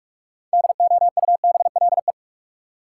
Morse code, heard as a keyed tone, about 45 wpm, 700 Hz. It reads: DOUBLE
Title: Sentences from Top 1400 Words - New Word Drill Alt - 45wpm